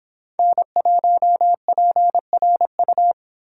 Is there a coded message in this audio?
Yes: Morse code from 0.4 to 3.1 s, heard as a keyed tone, about 26 wpm, 700 Hz. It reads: N1PRU